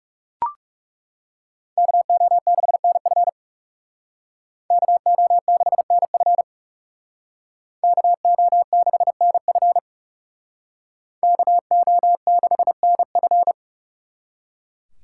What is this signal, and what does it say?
Morse code, heard as a keyed tone, about 35 wpm, 700 Hz. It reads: KO6NF KO6NF KO6NF KO6NF
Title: Speed Racing - USA 2x2 Call Signs 30wpm